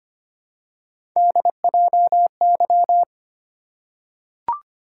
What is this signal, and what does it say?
Morse code, heard as a keyed tone, about 25 wpm, 700 Hz. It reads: DJY
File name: Three Letter-Number Combinations - 25wpm